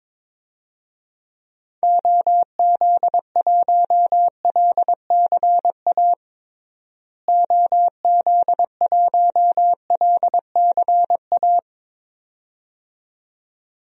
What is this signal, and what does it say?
Morse code, heard as a keyed tone, about 22 wpm, 700 Hz. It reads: OZ1LCA OZ1LCA
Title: DXCC-100 Call Signs - 6 Characters - Repeated 2x - 22wpm